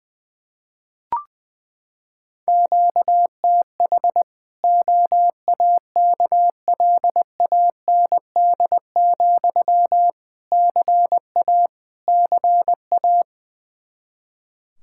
Morse code, heard as a keyed tone, about 20 wpm, 700 Hz. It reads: QTH OAKLAND, CA CA
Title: Practice QSO - Exchange 1 - Beginning - 20wpm